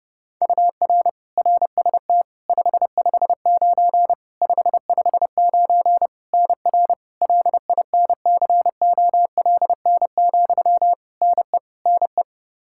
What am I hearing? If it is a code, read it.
Morse code, heard as a keyed tone, about 30 wpm, 700 Hz. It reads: UR RST 559 559 NR LINCOLN, NE NE
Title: Practice QSO - Exchange 1 - Beginning - 30wpm